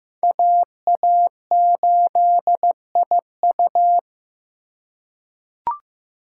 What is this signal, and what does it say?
Morse code, heard as a keyed tone, about 15 wpm, 700 Hz. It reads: AA8IU
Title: USA Call Signs - 5 Characters - 4x Character Spacing - 15wpm